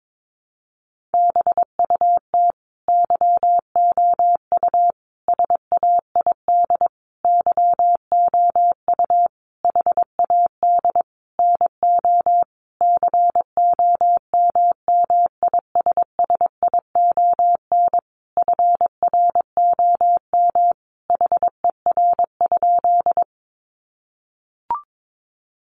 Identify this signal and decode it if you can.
Morse code, heard as a keyed tone, about 22 wpm, 700 Hz. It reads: BUT YOU SAID YOU HAD NO COMMISSION FROM HER?
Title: Sentences from Top 1400 Words 22wpm